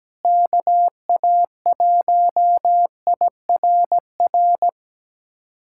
Morse code, heard as a keyed tone, about 17 wpm, 700 Hz. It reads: KA1IRR
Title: USA Callsign DE Callsign - 6 Characters - 17wpm